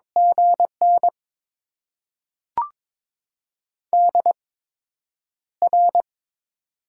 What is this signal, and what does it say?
Morse code, heard as a keyed tone, about 22 wpm, spaced as for 9 wpm, 700 Hz. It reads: GN DR
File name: Top 90 QSO Elements - 8x Character Spacing - 22wpm